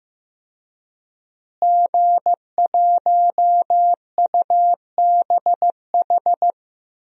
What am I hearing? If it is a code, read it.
Morse code, heard as a keyed tone, about 16 wpm, 700 Hz. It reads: G1UBH
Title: DXCC-100 Call Signs - 5 Characters - Repeated 3x - 15wpm